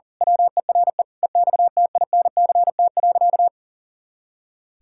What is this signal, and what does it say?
Morse code, heard as a keyed tone, about 37 wpm, 700 Hz. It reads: WERE EXTINCT.
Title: Pangram Challenge v2 - 40wpm